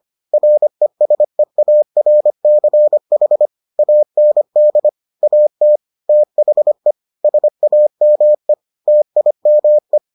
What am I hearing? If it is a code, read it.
Morse code, heard as a keyed tone, about 25 wpm, 600 Hz. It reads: RESEARCH AND AT THE SAME TIME